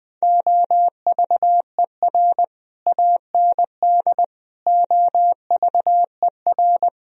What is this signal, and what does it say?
Morse code, heard as a keyed tone, about 20 wpm, 700 Hz. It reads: OVER AND OVER
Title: Speed-Racing - Sets of 4 Words 17wpm